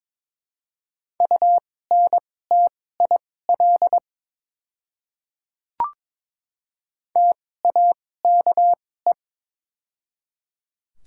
Morse code, heard as a keyed tone, about 22 wpm, spaced as for 16 wpm, 700 Hz. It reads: UNTIL TAKE
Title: 2x Character Spacing - Top 300 Words - 22wpm